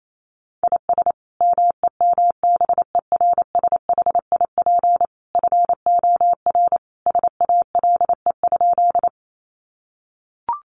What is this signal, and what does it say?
Morse code, heard as a keyed tone, about 28 wpm, 700 Hz. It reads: IS MEMBERSHIP FOR SALE?
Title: Sentences from Top 1400 Words 28wpm